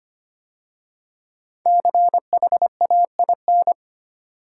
Morse code, heard as a keyed tone, about 25 wpm, 700 Hz. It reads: CHAIN
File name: 5 Letter Words - First 3 Letters Given - 25wpm